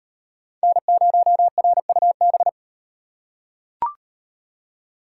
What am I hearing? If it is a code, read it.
Morse code, heard as a keyed tone, about 34 wpm, 700 Hz. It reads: N0RUB